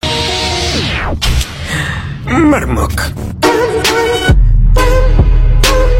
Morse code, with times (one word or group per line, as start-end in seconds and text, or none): none